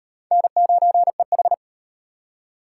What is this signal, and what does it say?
Morse code, heard as a keyed tone, about 36 wpm, 700 Hz. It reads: N9EH